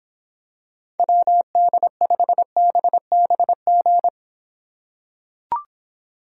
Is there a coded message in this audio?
Yes: Morse code from 1.0 to 4.1 s, heard as a keyed tone, about 25 wpm, 700 Hz. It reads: WD5BBG